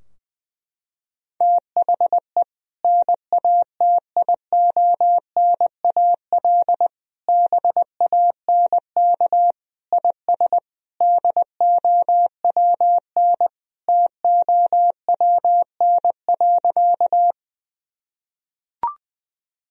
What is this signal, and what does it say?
Morse code, heard as a keyed tone, about 21 wpm, 700 Hz. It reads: THE NATIONAL BANK IS DOWN TOWN.
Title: Sentences from Top 1000 Words - Review Part 4 - 20wpm